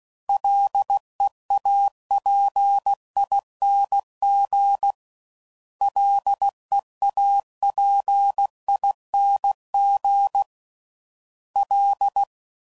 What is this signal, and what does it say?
Morse code, heard as a keyed tone, about 16 wpm, 800 Hz. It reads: LEAPING LEAPING L